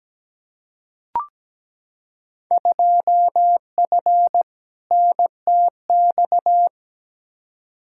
Morse code, heard as a keyed tone, about 17 wpm, 700 Hz. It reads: 2F NTX
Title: Field Day Exchanges - 17wpm